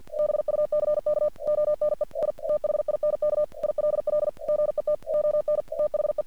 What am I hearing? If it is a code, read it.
Morse code, heard as a keyed tone, about 48 wpm, 600 Hz. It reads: BUCK ONE I THINK ILL GET ON THE
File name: archived / CRSnet